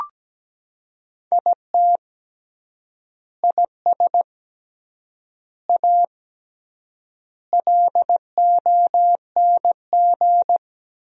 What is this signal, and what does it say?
Morse code, heard as a keyed tone, about 17 wpm, 700 Hz. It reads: IT IS A LONG
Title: 3x Word-Spacing - Sentences from Top 300 Words - 17wpm